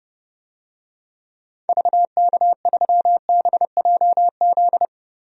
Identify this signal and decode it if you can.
Morse code, heard as a keyed tone, about 30 wpm, 700 Hz. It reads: VK3BJZ
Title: DXCC-100 Call Signs - 6 Characters - Repeated 3x - 30wpm